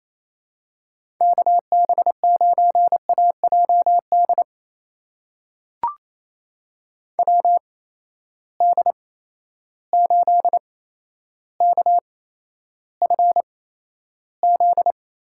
Morse code, heard as a keyed tone, about 28 wpm, spaced as for 12 wpm, 700 Hz. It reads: KB9AJD WD8KFZ
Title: USA Call Signs - 6 Characters - 8x Character Spacing - 28wpm